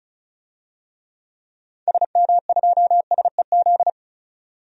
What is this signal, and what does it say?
Morse code, heard as a keyed tone, about 36 wpm, 700 Hz. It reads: SM2SEZ
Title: DXCC-100 Call Signs - 6 Characters - 35wpm